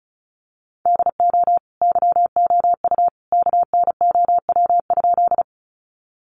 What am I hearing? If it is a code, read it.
Morse code, heard as a keyed tone, about 34 wpm, 700 Hz. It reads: DO YOU KNOW?
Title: Sentences from Top 100 Words 35wpm